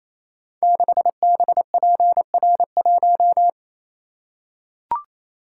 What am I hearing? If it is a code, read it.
Morse code, heard as a keyed tone, about 28 wpm, 700 Hz. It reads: 6BPR1
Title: Code Groups - 5 Letter:Numbers - 28wpm